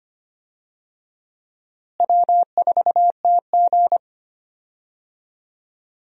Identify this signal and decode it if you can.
Morse code, heard as a keyed tone, about 25 wpm, 700 Hz. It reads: W4TG
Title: USA Call Signs - 4 Characters - Repeated 3x - 25wpm